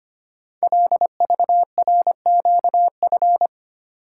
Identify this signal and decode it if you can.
Morse code, heard as a keyed tone, about 25 wpm, 700 Hz. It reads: LVRQF
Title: Code Groups - 5 Letters - 25wpm